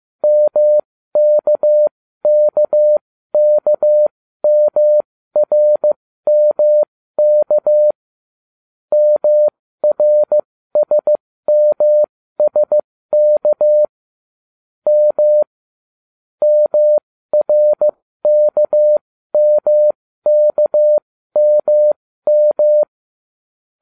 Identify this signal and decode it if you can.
Morse code, heard as a keyed tone, about 15 wpm, spaced as for 13 wpm, 600 Hz. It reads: MKKKMRMK MRSMSK M MRKMKMM